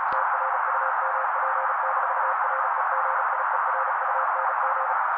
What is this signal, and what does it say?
Morse code, heard as a keyed tone, about 36 wpm, 530 Hz. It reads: TRUMP-REGIERUNG
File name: daily / 35wpm noise